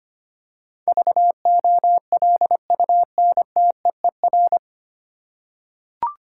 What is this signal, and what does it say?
Morse code, heard as a keyed tone, about 25 wpm, 700 Hz. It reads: VOLUNTEER
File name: Nine Letter Words - 25wpm